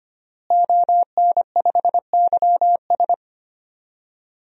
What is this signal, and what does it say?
Morse code, heard as a keyed tone, about 25 wpm, 700 Hz. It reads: ON5YS
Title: DXCC-100 Call Signs - 5 Characters - Repeated 3x - 25wpm